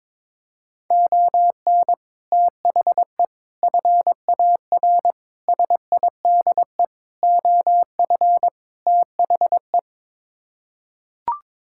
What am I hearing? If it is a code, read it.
Morse code, heard as a keyed tone, about 22 wpm, 700 Hz. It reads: ON THE FAR SIDE OF THE
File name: practice / 22wpm / Sets of 6 Words - 22wpm